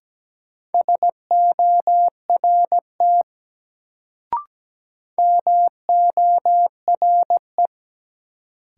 Morse code, heard as a keyed tone, about 17 wpm, 700 Hz. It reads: SORT MORE
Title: Sets of 2 Words - New Word Drill - 17wpm